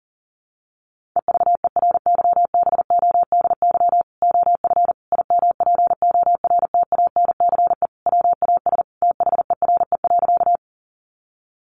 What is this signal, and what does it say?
Morse code, heard as a keyed tone, about 40 wpm, 700 Hz. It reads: EVERYBODY OF IMPORTANCE WAS THERE.
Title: Sentences from Top 1400 Words 40wpm